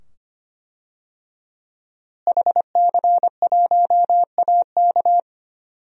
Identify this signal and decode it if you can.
Morse code, heard as a keyed tone, about 25 wpm, 700 Hz. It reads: HC1AK
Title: DXCC-100 Call Signs - 5 Characters - Repeated 2x - 25wpm